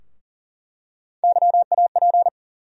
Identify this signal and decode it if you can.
Morse code, heard as a keyed tone, about 40 wpm, 700 Hz. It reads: YAP
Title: USA Call Sign Postfixes - 3 Letters 40wpm